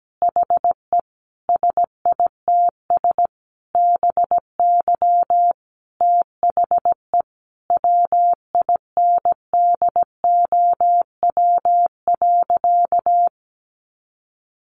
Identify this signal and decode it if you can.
Morse code, heard as a keyed tone, about 17 wpm, 700 Hz. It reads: HE SITS BY THE WINDOW.
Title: Easy Phrases 17wpm